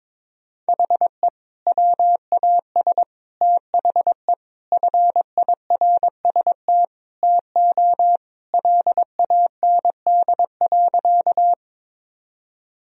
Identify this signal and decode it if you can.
Morse code, heard as a keyed tone, about 22 wpm, 700 Hz. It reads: HE WAS THE FIRST TO LAND.
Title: Sentences from Top 1000 Words - Review Part 1 - 22wpm